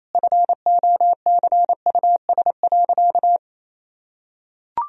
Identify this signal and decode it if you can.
Morse code, heard as a keyed tone, about 28 wpm, 700 Hz. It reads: FOCUS.
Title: Sentences from Top 1000 Words - Review Part 4 - 28wpm